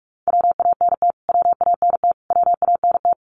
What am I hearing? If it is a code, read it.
Morse code, heard as a keyed tone, about 44 wpm, 700 Hz. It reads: WANT WANT WANT
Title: Top 500 Words Repeated 45wpm